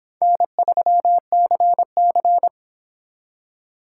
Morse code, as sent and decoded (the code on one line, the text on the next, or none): -. ...-- -.-. -.-.
N3CC